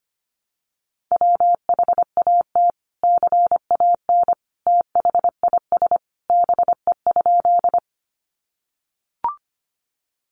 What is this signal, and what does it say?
Morse code, heard as a keyed tone, about 25 wpm, 700 Hz. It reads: WHAT CAN THIS BE?
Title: Sentences from Top 100 Words 25wpm